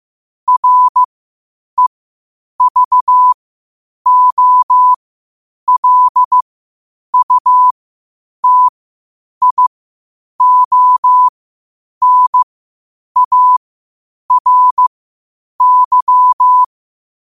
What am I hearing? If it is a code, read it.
Morse code, heard as a keyed tone, about 15 wpm, spaced as for 8 wpm, 1000 Hz. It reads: REVOLUTIONARY
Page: 8 wpm morse code quotes